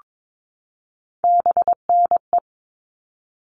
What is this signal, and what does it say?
Morse code, heard as a keyed tone, about 22 wpm, 700 Hz. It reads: BNE
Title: All Cut Numbers - 3-Digit 22wpm